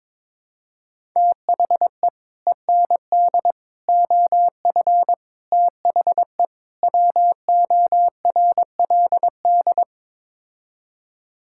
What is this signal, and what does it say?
Morse code, heard as a keyed tone, about 22 wpm, 700 Hz. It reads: THE END OF THE WORLD
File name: practice / 22wpm / Build A Phrase - 5-Word Phrases - 22wpm